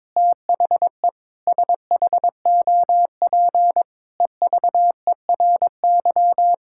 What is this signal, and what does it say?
Morse code, heard as a keyed tone, about 22 wpm, 700 Hz. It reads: THE SHOP EVERY
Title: Sentences from Top 1600 Words 22wpm